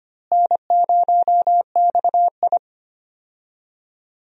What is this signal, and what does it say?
Morse code, heard as a keyed tone, about 25 wpm, 700 Hz. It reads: N0XI
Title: USA Call Signs - 4 Characters - Repeated 3x - 25wpm